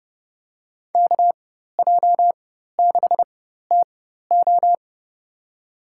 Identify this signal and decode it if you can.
Morse code, heard as a keyed tone, about 30 wpm, spaced as for 14 wpm, 700 Hz. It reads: KJ6TO
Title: USA Call Signs - 5 Characters - 4x Character Spacing - 30wpm